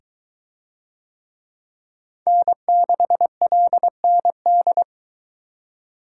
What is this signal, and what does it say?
Morse code, heard as a keyed tone, about 21 wpm, 700 Hz. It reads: N6LND